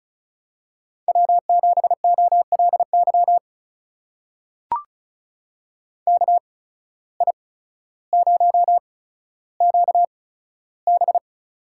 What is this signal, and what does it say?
Morse code, heard as a keyed tone, about 36 wpm, spaced as for 15 wpm, 700 Hz. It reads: W7OLY KI0QB
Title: USA Call Signs - 8x Character Spacing - 35wpm